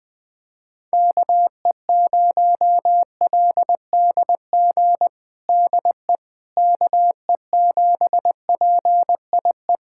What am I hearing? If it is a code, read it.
Morse code, heard as a keyed tone, about 21 wpm, 700 Hz. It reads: KE0LDG DE KE7PIE
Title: USA Callsign DE Callsign - 6 Characters - 20wpm